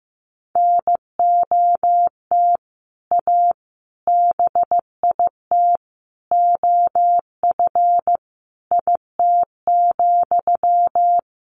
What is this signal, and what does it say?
Morse code, heard as a keyed tone, about 15 wpm, 700 Hz. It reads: NOT A BIT OF IT,
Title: Sentences from Top 1400 Words 15wpm